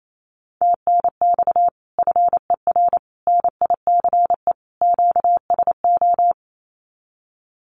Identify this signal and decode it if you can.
Morse code, heard as a keyed tone, about 28 wpm, 700 Hz. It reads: TNX FER NICE QSO
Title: Common QSO Phrases Encore 28wpm